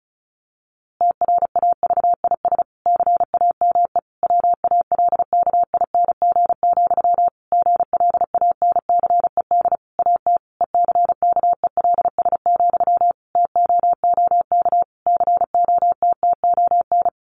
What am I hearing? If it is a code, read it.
Morse code, heard as a keyed tone, about 34 wpm, 700 Hz. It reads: TRAVIS CAME WALKING, GLANCED AT ECKELS, TOOK COTTON
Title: A Sound Of Thunder 35wpm